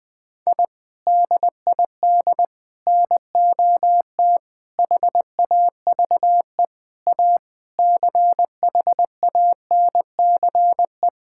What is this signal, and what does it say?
Morse code, heard as a keyed tone, about 21 wpm, 700 Hz. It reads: I DID NOT HAVE A CHANCE
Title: Sentences with 5-Word Phrases - Spoken Beginning - 20wpm